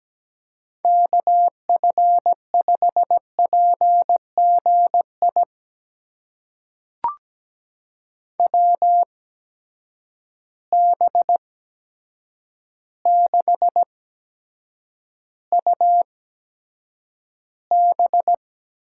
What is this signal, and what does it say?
Morse code, heard as a keyed tone, about 17 wpm, spaced as for 8 wpm, 700 Hz. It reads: KF5PGI WB6UB
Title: USA Call Signs - 6 Characters - 8x Character Spacing - 17wpm